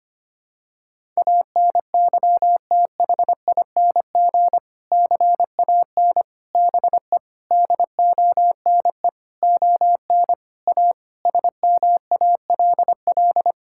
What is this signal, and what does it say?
Morse code, heard as a keyed tone, about 25 wpm, 700 Hz. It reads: ANYTHING CAN BE DONE ON A SMALL
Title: Sentences from Top 1000 Words - Review Part 5 - 25wpm